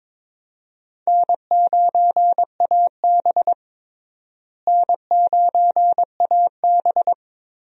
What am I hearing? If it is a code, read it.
Morse code, heard as a keyed tone, about 22 wpm, 700 Hz. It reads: N9AB N9AB
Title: USA Call Signs - 4 Characters - Repeated 2x - 22wpm